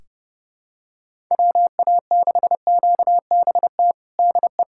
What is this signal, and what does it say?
Morse code, heard as a keyed tone, about 32 wpm, 700 Hz. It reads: WA6QBT DE
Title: USA Callsign DE Callsign 30wpm